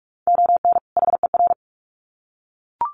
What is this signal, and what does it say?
Morse code, heard as a keyed tone, about 44 wpm, 700 Hz. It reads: ON HER
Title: Sets of 2 Words 45wpm